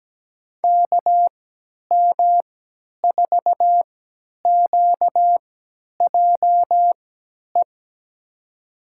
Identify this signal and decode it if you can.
Morse code, heard as a keyed tone, about 17 wpm, spaced as for 10 wpm, 700 Hz. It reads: KM4QJE
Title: USA Call Signs - 3x Character Spacing - 17wpm